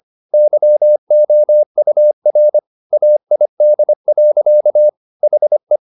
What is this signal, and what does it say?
Morse code, heard as a keyed tone, about 25 wpm, 600 Hz. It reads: YOUR AID. HE
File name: the old man and the sea part 1 25wpm